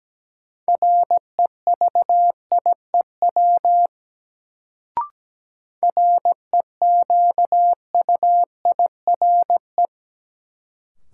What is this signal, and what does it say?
Morse code, heard as a keyed tone, about 17 wpm, 700 Hz. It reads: REVIEW REQUIRE
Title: Sentences from Top 900 Words - New Word Drill - 17wpm